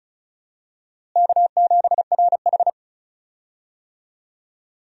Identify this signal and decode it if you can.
Morse code, heard as a keyed tone, about 36 wpm, 700 Hz. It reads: K7RH
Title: USA Call Signs - 4 Characters - Repeated 3x - 35wpm